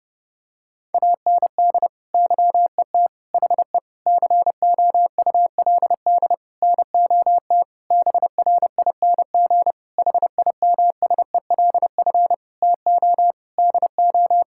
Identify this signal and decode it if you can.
Morse code, heard as a keyed tone, about 30 wpm, 700 Hz. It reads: AND YET HE COULD NOT BRING HIMSELF TO DO
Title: Sentences from Top 1600 Words 30wpm